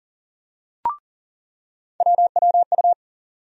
Morse code, heard as a keyed tone, about 40 wpm, 700 Hz. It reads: WWU
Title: USA Call Sign Postfixes - 3 Letters 40wpm